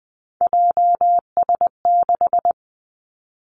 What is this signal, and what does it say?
Morse code, heard as a keyed tone, about 20 wpm, 700 Hz. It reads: JS6
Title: Three Challenging Letter-Number Combinations 20wpm